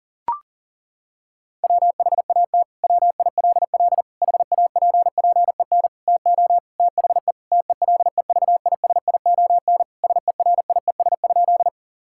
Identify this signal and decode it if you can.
Morse code, heard as a keyed tone, about 40 wpm, 700 Hz. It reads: WHAT WILL HAPPEN TO THE TELEVISION SERIES?
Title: Sentences from Top 1000 Words - Review Part 5 - 40wpm